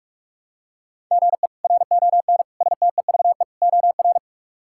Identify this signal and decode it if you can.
Morse code, heard as a keyed tone, about 46 wpm, 700 Hz. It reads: GE RON STEVE OR
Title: K1USN SST QSO Practice - 45wpm